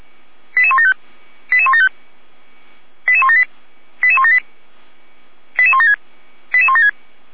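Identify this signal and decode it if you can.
Five-tone selective call (ZVEI), heard as quick runs of five tones, tones of about 80 ms. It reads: 88175 88175 88178 88178 88176 88176
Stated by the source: ZVEI five-tone call 88171 received by SDR.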